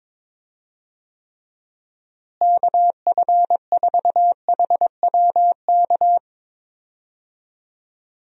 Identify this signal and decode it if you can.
Morse code, heard as a keyed tone, about 22 wpm, 700 Hz. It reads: KF4HWK
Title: USA Call Signs - 6 Characters - Repeated 2x - 22wpm